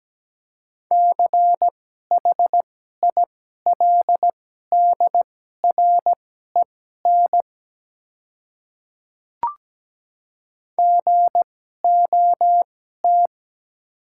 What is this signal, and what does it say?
Morse code, heard as a keyed tone, about 17 wpm, spaced as for 12 wpm, 700 Hz. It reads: CHILDREN GOT
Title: 2x Character Spacing - Top 300 Words - 17wpm